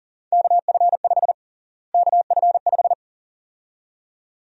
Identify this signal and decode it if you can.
Morse code, heard as a keyed tone, about 37 wpm, 700 Hz. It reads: KF5 KF5
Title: USA Call Sign Prefixes - Repeated 2x - 40wpm